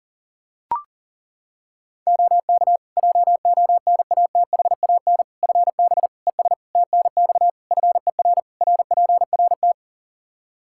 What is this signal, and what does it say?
Morse code, heard as a keyed tone, about 40 wpm, 700 Hz. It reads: OK JONATHAN FB ES TNX FER RPRT
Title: Practice QSO - Exchange 2 - Info - 40wpm